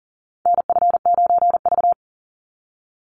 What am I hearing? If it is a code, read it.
Morse code, heard as a keyed tone, about 40 wpm, 700 Hz. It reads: NF9V
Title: Call Signs - 40wpm